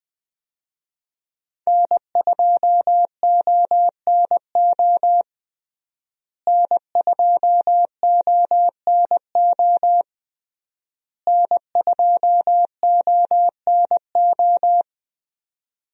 Morse code, heard as a keyed tone, about 21 wpm, 700 Hz. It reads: N2ONO N2ONO N2ONO
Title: USA Call Signs - 5 Characters - Repeated 3x - 20wpm